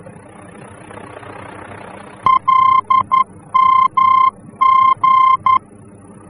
Morse code, heard as a keyed tone, about 11 wpm, 1050 Hz. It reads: LMG
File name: france / mx / LMG-F-VOR